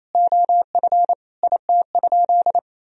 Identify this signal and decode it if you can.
Morse code, heard as a keyed tone, about 28 wpm, 700 Hz. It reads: OF IT?
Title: Sentences from Top 1000 Words - Review Part 1 - 28wpm